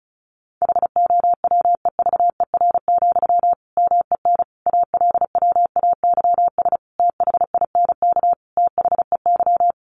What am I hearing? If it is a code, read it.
Morse code, heard as a keyed tone, about 34 wpm, 700 Hz. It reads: HOWEVER, MEN ALWAYS THINK THEY
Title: Sentences from Top 600 Words 35wpm